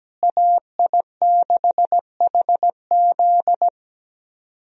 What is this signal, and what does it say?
Morse code, heard as a keyed tone, about 17 wpm, 700 Hz. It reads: AI6HZ
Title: USA Call Signs - 5 Characters - Repeated 3x - 17wpm